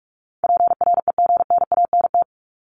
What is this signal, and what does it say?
Morse code, heard as a keyed tone, about 44 wpm, 700 Hz. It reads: PREGNANT
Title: Top 751-1000 Words 45wpm